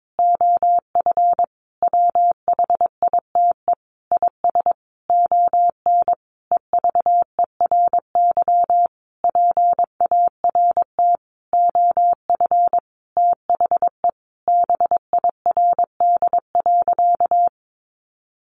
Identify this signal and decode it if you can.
Morse code, heard as a keyed tone, about 22 wpm, 700 Hz. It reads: OF WHITE IS ON EVERY PART OF THE BIRD.
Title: Sentences from Top 1500 Words 22wpm